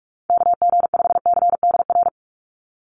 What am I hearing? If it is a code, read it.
Morse code, heard as a keyed tone, about 44 wpm, 700 Hz. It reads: KG5CDR
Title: USA Call Signs 45wpm